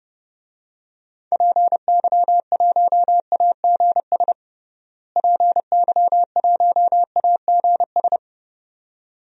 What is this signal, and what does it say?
Morse code, heard as a keyed tone, about 31 wpm, 700 Hz. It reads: PY1AGS PY1AGS
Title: DXCC-100 Call Signs - 6 Characters - Repeated 2x - 30wpm